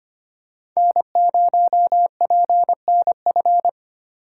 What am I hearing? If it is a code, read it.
Morse code, heard as a keyed tone, about 25 wpm, 700 Hz. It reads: N0PNF
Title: USA Call Signs - 5 Characters - 3x Character Spacing - 25wpm